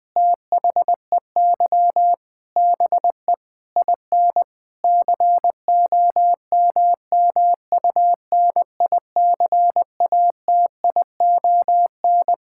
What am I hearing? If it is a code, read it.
Morse code, heard as a keyed tone, about 20 wpm, 700 Hz. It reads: THEY BE IN COMMUNICATION
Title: Sentences from Top 1000 Words - Review Part 6 - 20wpm